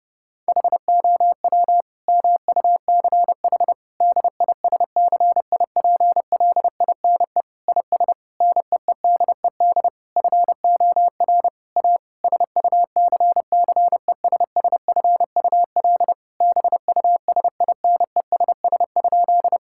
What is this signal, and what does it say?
Morse code, heard as a keyed tone, about 30 wpm, 700 Hz. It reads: HOW MUCH DISCIPLINE IS NEEDED FOR A SUCCESSFUL BUSINESS?
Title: Sentences from Top 1000 Words - Review Part 6 - 30wpm